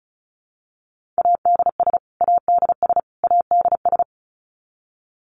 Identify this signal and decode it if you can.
Morse code, heard as a keyed tone, about 34 wpm, 700 Hz. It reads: ADS ADS ADS
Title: Abbreviations Repeated 3x - 35wpm